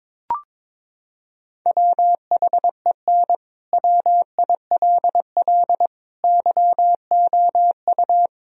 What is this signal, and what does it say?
Morse code, heard as a keyed tone, about 22 wpm, 700 Hz. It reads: WHEN WILL YOU
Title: Sentences from Top 1000 Words - Review Part 1 - 22wpm